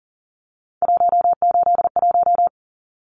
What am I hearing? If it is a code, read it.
Morse code, heard as a keyed tone, about 40 wpm, 700 Hz. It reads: 181
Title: Numbers - 3-Digit 40wpm